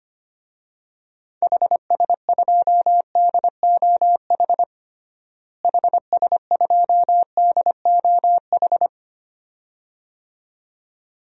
Morse code, heard as a keyed tone, about 25 wpm, 700 Hz. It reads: HS2DOH HS2DOH
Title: DXCC-100 Call Signs - 6 Characters - Repeated 2x - 25wpm